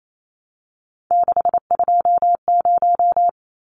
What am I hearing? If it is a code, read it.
Morse code, heard as a keyed tone, about 28 wpm, 700 Hz. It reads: 620
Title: Numbers - 3-Digit 28wpm